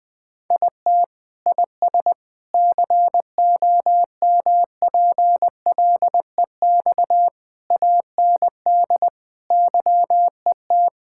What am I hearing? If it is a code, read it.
Morse code, heard as a keyed tone, about 20 wpm, 700 Hz. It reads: IT IS COMPLEX AND YET
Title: Sentences from Top 1000 Words - Review Part 6 - 20wpm